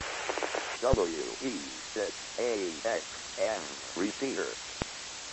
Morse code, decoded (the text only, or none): none